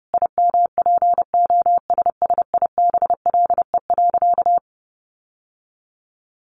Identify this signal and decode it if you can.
Morse code, heard as a keyed tone, about 28 wpm, 700 Hz. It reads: IMPOSSIBLE.
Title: Sentences from Top 1100 Words 30wpm